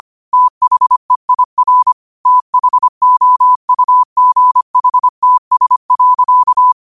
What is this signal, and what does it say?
Morse code, heard as a keyed tone, about 25 wpm, 1000 Hz. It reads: THEIR THOUGHTS.
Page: New quotes every day in morse code at 25 Words per minute.